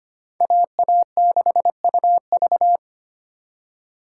Morse code, heard as a keyed tone, about 32 wpm, 700 Hz. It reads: AA6UV